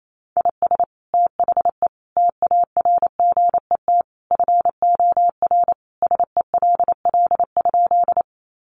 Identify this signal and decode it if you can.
Morse code, heard as a keyed tone, about 28 wpm, 700 Hz. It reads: IS THE TARGET FOR SELL?
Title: Sentences from Top 1200 Words 28wpm